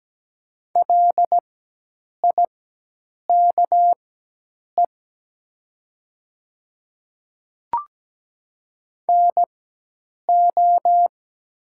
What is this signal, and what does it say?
Morse code, heard as a keyed tone, about 17 wpm, spaced as for 8 wpm, 700 Hz. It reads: LIKE NO
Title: 4x Character Spacing - Top 200 Words - 17wpm